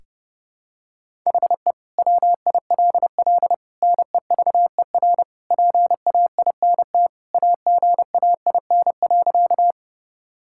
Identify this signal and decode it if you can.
Morse code, heard as a keyed tone, about 32 wpm, 700 Hz. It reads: HE WILL NEVER PAINT AGAIN.
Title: Sentences from Top 1000 Words - Review Part 6 - 30wpm